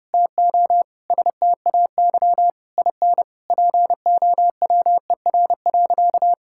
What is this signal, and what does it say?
Morse code, heard as a keyed tone, about 30 wpm, 700 Hz. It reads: TO STAY IN POWER.
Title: Sentences from Top 1000 Words - Review Part 5 - 30wpm